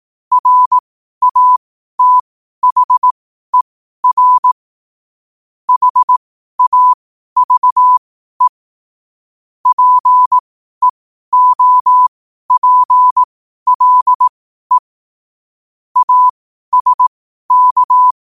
Morse code, heard as a keyed tone, about 18 wpm, spaced as for 12 wpm, 1000 Hz. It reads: RATHER HAVE PEOPLE ASK
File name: qotdfiles / 12wpm